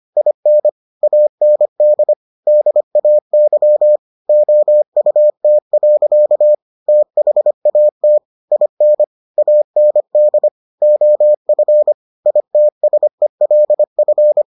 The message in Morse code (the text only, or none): IN AND DAY OUT. THAT IN AND OF ITSELF